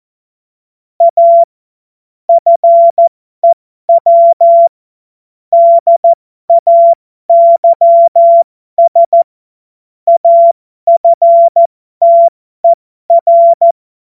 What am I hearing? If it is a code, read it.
Morse code, heard as a keyed tone, about 14 wpm, spaced as for 12 wpm, 680 Hz. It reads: A FEW DAYS AFTER
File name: Morsekurs / Witze